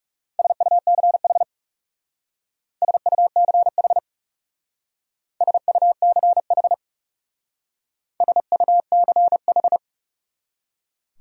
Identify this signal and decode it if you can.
Morse code, heard as a keyed tone, about 38 wpm, 700 Hz. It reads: SUCH SUCH SUCH SUCH